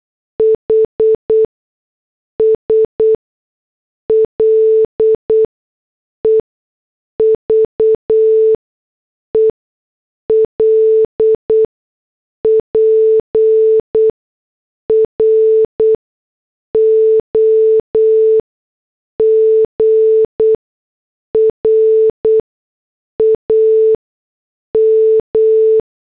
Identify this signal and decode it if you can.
Morse code, heard as a keyed tone, about 8 wpm, spaced as for 6 wpm, 440 Hz. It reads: HS LEVEL PROGRAM